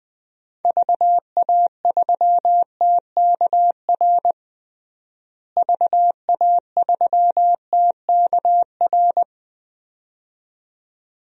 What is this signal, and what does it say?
Morse code, heard as a keyed tone, about 20 wpm, 700 Hz. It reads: VA3TKR VA3TKR
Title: DXCC-100 Call Signs - 6 Characters - Repeated 2x - 20wpm